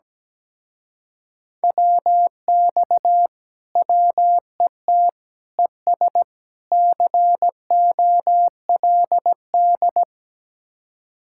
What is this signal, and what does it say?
Morse code, heard as a keyed tone, about 17 wpm, 700 Hz. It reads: WX WET ES COLD
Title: Speed-Racing Common QSO Phrases Encore - 17wpm